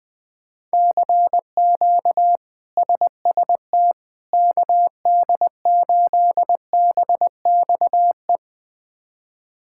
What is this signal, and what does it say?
Morse code, heard as a keyed tone, about 20 wpm, 700 Hz. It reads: CQ SST KD8BXE
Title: K1USN SST QSO Practice - 20wpm